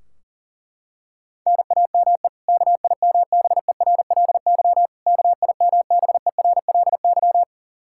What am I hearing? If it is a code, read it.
Morse code, heard as a keyed tone, about 40 wpm, 700 Hz. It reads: NAME KIMBERLY KIMBERLY
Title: Practice QSO - Exchange 1 - Beginning - 40wpm